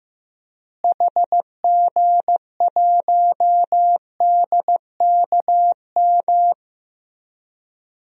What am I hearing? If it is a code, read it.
Morse code, heard as a keyed tone, about 15 wpm, 700 Hz. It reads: HG1DKM
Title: DXCC-100 Call Signs - 6 Characters - Repeated 3x - 15wpm